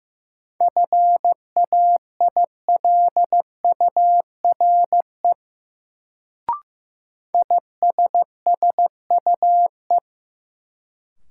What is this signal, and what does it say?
Morse code, heard as a keyed tone, about 15 wpm, 700 Hz. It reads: FAILURE ISSUE
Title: Sentences from Top 900 Words - New Word Drill - 15wpm